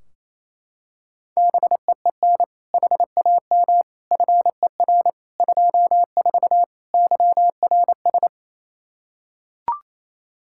Morse code, heard as a keyed tone, about 28 wpm, 700 Hz. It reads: BEEN HAM FER 24 YRS
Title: Practice QSO - Exchange 3 - More Info - 28wpm